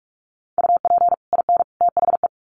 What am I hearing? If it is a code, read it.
Morse code, heard as a keyed tone, about 44 wpm, 700 Hz. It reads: UP IN THE
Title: Sets of 3 Words 45wpm